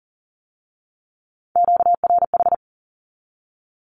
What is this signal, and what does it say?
Morse code, heard as a keyed tone, about 39 wpm, 700 Hz. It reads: QRH
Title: Q-Codes with Spoken Meaning - 40wpm